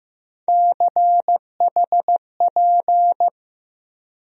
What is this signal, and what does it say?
Morse code, heard as a keyed tone, about 15 wpm, 700 Hz. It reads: CHP
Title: USA Call Sign Postfixes - 3 Letters 15wpm